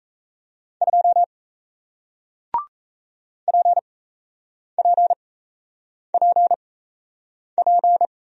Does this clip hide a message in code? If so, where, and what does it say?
Yes: Morse code from 0.8 to 8.1 s, heard as a keyed tone, about 37 wpm, spaced as for 8 wpm, 700 Hz. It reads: 2 PPPP